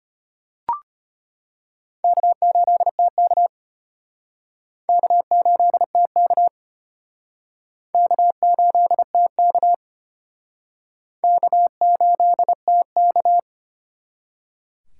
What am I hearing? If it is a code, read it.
Morse code, heard as a keyed tone, about 30 wpm, 700 Hz. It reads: K8TK K8TK K8TK K8TK
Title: Speed Racing - USA Call Signs 25wpm